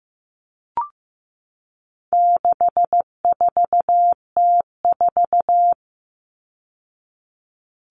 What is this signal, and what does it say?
Morse code, heard as a keyed tone, about 15 wpm, 700 Hz. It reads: 64T4
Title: All Cut Numbers - 4-Digit 15wpm